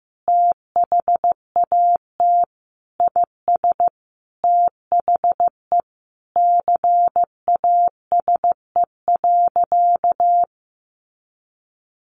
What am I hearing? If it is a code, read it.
Morse code, heard as a keyed tone, about 15 wpm, 700 Hz. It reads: THAT IS THE CASE.
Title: Sentences from Top 1200 Words 15wpm